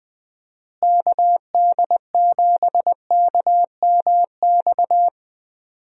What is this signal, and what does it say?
Morse code, heard as a keyed tone, about 20 wpm, 700 Hz. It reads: KD7KMX
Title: USA Call Signs - 6 Characters - Repeated 2x - 20wpm